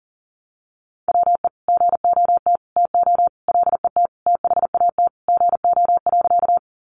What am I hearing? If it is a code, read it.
Morse code, heard as a keyed tone, about 40 wpm, 700 Hz. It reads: WE GOT TO LET THAT GO.
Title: Sentences from Top 300 Words 40wpm